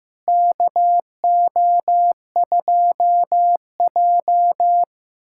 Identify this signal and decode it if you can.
Morse code, heard as a keyed tone, about 15 wpm, 700 Hz. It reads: KO2J
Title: USA Callsign DE Callsign - 4 Characters - 15wpm